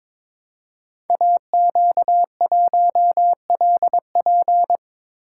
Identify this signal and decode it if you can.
Morse code, heard as a keyed tone, about 22 wpm, 700 Hz. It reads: AQ1LP
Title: Code Groups - 5 Letter:Numbers - 22wpm